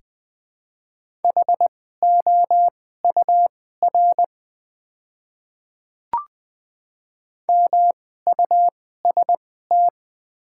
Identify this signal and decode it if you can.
Morse code, heard as a keyed tone, about 20 wpm, spaced as for 15 wpm, 700 Hz. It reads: HOUR MUST
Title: 2x Character Spacing - Top 500 Words - 20wpm